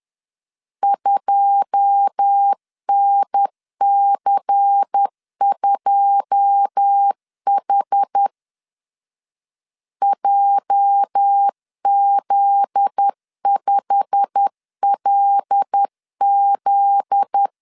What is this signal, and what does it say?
Morse code, heard as a keyed tone, about 10 wpm, 800 Hz. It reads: 2NC2H JZ5LZ
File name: LEZIONI / 9 2 10wpm 2736